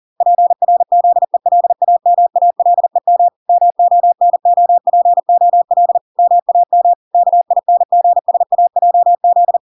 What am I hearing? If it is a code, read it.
Morse code, heard as a keyed tone, about 40 wpm, 700 Hz. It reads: PRZELAMALEM MONOPOL MAM KINGSAJZ